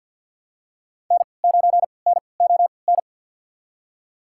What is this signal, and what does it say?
Morse code, heard as a keyed tone, about 53 wpm, spaced as for 29 wpm, 700 Hz. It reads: N9NON
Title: USA Call Signs - 5 Characters - 3x Character Spacing - 50wpm